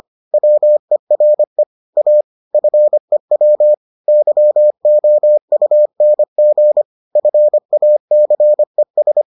Morse code, heard as a keyed tone, about 25 wpm, 600 Hz. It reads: WERE A FEW YOUNG FACES